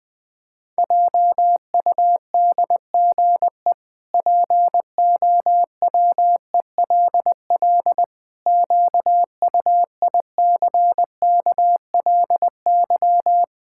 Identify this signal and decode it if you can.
Morse code, heard as a keyed tone, about 20 wpm, 700 Hz. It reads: JUDGE POWELL QUICKLY
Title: Pangram Challenge v2 - 20wpm